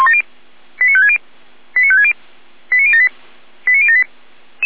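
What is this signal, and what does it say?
Five-tone selective call (ZVEI), heard as quick runs of five tones, tones of about 70 ms. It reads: 89560 89560 89078 89078
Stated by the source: ZVEI five-tone call received by SDR, recorded by feuerwehr-orlamuende (fire brigade Orlamünde).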